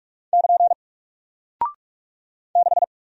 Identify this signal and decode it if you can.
Morse code, heard as a keyed tone, about 44 wpm, 700 Hz. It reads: <KN> 6
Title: Single Character (Letter:Number:Punctuation:Prosign) Alt 45wpm